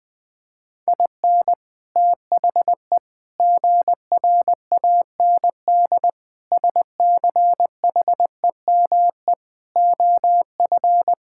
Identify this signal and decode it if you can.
Morse code, heard as a keyed tone, about 20 wpm, 700 Hz. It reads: IN THE GRAND SCHEME OF
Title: Sentences with 6-Word Phrases - Spoken Beginning - 20wpm